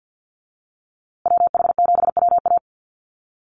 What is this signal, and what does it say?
Morse code, heard as a keyed tone, about 50 wpm, 700 Hz. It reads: WH7WA
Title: USA Call Signs Repeated 2x 50wpm